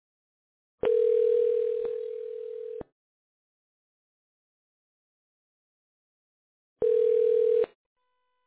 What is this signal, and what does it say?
Ringback tone (North America), 440 + 480 Hz.